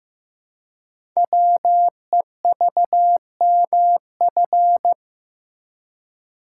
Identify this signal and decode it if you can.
Morse code, heard as a keyed tone, about 15 wpm, 700 Hz. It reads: WEVMF
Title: Code Groups - 5 Letters - 15wpm